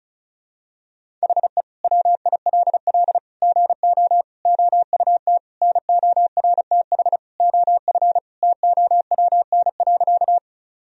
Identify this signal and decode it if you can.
Morse code, heard as a keyed tone, about 36 wpm, 700 Hz. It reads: HE WILL GO OUT NORTH OF TOWN.
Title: Sentences from Top 1000 Words - Review Part 1 - 35wpm